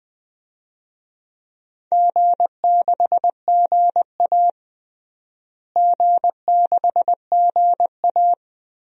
Morse code, heard as a keyed tone, about 20 wpm, 700 Hz. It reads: G6GA G6GA
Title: DXCC-100 Call Signs - 4 Characters - Repeated 3x - 20wpm